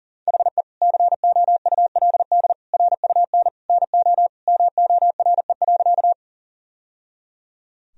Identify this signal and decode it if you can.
Morse code, heard as a keyed tone, about 40 wpm, 700 Hz. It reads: HE COULD RUN NO MORE.
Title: Sentences from Top 1000 Words - Review Part 1 - 40wpm